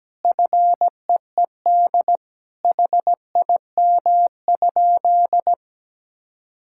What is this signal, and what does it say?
Morse code, heard as a keyed tone, about 17 wpm, 700 Hz. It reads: FEED HIM?
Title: Sentences from Top 1600 Words 17wpm